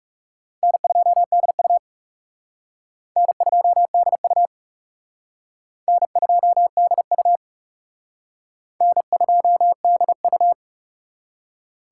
Morse code, heard as a keyed tone, about 36 wpm, 700 Hz. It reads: N2DU N2DU N2DU N2DU
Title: Speed Racing - USA 1x2 Call Signs 30wpm